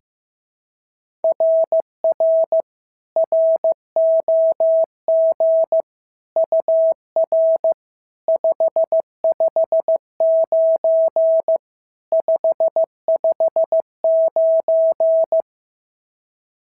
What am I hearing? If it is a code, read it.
Morse code, heard as a keyed tone, about 14 wpm, 650 Hz. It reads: RR ROG UR 559 559
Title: SOTA QSO Practice 15wpm